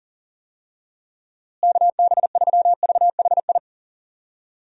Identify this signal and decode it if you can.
Morse code, heard as a keyed tone, about 39 wpm, 700 Hz. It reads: KB3VHI
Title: USA Call Signs - 6 Characters - Repeated 2x - 40wpm